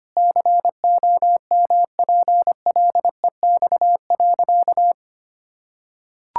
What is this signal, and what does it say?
Morse code, heard as a keyed tone, about 25 wpm, 700 Hz. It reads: COMPLEX.
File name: practice / 25wpm / Sentences with 4-Word Phrases - 25wpm